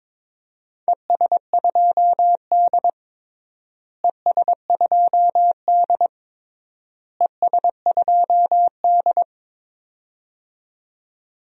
Morse code, heard as a keyed tone, about 22 wpm, 700 Hz. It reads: ES2D ES2D ES2D
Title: DXCC-100 Call Signs - 4 Characters - Repeated 3x - 22wpm